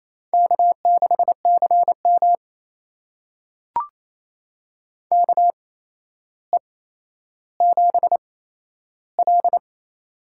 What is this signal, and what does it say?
Morse code, heard as a keyed tone, about 28 wpm, spaced as for 12 wpm, 700 Hz. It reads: K6CM KE7L
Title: USA Call Signs - 4 Characters - 8x Character Spacing - 28wpm